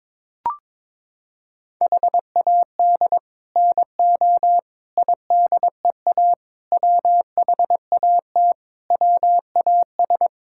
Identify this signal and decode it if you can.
Morse code, heard as a keyed tone, about 22 wpm, 700 Hz. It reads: HAD NO IDEA WHAT WAS
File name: practice / 22wpm / Sets of 7 Words - 22wpm